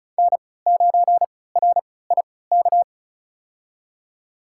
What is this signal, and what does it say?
Morse code, heard as a keyed tone, about 36 wpm, spaced as for 20 wpm, 700 Hz. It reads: N9RIK
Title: USA Call Signs - 5 Characters - 3x Character Spacing - 35wpm